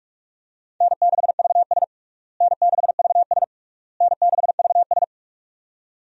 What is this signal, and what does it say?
Morse code, heard as a keyed tone, about 46 wpm, 700 Hz. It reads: N6VS N6VS N6VS
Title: USA Call Signs - 4 Characters - Repeated 3x - 45wpm